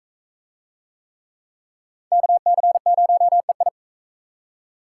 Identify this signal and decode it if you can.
Morse code, heard as a keyed tone, about 37 wpm, 700 Hz. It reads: KC0EI